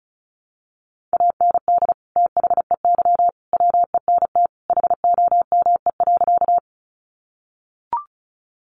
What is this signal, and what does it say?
Morse code, heard as a keyed tone, about 34 wpm, 700 Hz. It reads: AND THEY WENT HOME.
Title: Sentences from Top 200 Words 35wpm